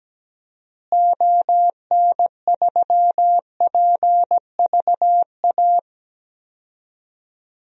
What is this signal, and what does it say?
Morse code, heard as a keyed tone, about 17 wpm, 700 Hz. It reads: ON3PVA
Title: DXCC-100 Call Signs - 6 Characters - Repeated 2x - 17wpm